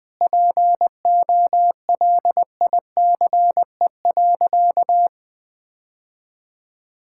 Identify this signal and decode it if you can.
Morse code, heard as a keyed tone, about 20 wpm, 700 Hz. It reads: POLICE.
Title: Sentences with 6-Word Phrases - Spoken Beginning - 20wpm